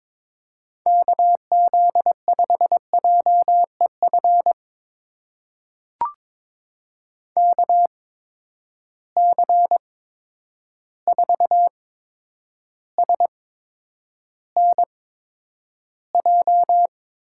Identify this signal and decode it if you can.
Morse code, heard as a keyed tone, about 22 wpm, spaced as for 9 wpm, 700 Hz. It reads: KZ5JEF KC4SNJ
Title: USA Call Signs - 6 Characters - 8x Character Spacing - 22wpm